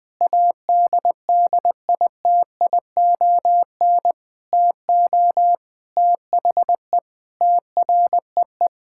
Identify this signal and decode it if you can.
Morse code, heard as a keyed tone, about 20 wpm, 700 Hz. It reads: ADDITION TO THE TREE
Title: Sentences from Top 1000 Words - Review Part 6 - 20wpm